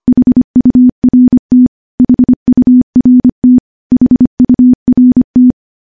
Morse code, heard as a keyed tone, about 25 wpm, 250 Hz. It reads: HURT HURT HURT
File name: morse-training-25WPM